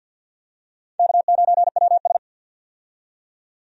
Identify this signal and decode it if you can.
Morse code, heard as a keyed tone, about 47 wpm, 700 Hz. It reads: K9WS
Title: USA Call Signs - 4 Characters - 50wpm